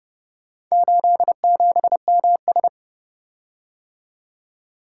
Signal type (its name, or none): Morse code